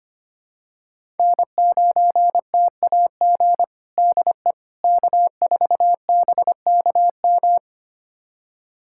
Morse code, heard as a keyed tone, about 25 wpm, 700 Hz. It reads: N9TAG DE K4BKM
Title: USA Callsign DE Callsign - 5 Characters - 25wpm